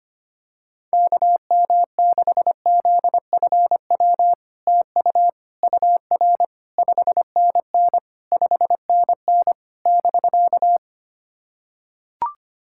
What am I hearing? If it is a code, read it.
Morse code, heard as a keyed tone, about 25 wpm, 700 Hz. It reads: KM6ZFW TU UR 5NN 5NN <BK>
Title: SOTA QSO Practice 25wpm